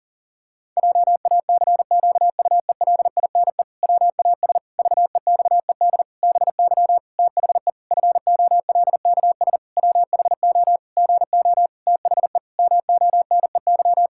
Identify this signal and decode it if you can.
Morse code, heard as a keyed tone, about 40 wpm, 700 Hz. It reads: JACQUELINE WAS VEXED BY THE FOLKS WHO GO THE MONEY
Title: Pangram Challenge v2 - 40wpm